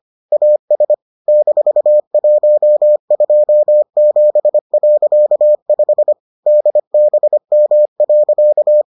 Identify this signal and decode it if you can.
Morse code, heard as a keyed tone, about 25 wpm, 600 Hz. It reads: AS -127.5 DBM.